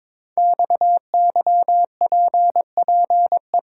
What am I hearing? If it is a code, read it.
Morse code, heard as a keyed tone, about 22 wpm, 700 Hz. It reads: XYPPE
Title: Code Groups - 5 Letters - 22wpm